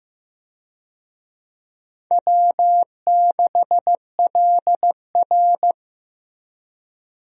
Morse code, heard as a keyed tone, about 15 wpm, 700 Hz. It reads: W6LR
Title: USA Call Signs - 4 Characters - Repeated 3x - 15wpm